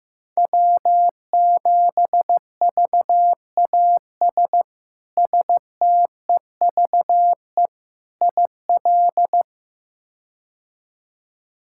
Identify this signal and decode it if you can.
Morse code, heard as a keyed tone, about 15 wpm, 700 Hz. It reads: W7VAS STEVE IL
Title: K1USN SST QSO Practice - 15wpm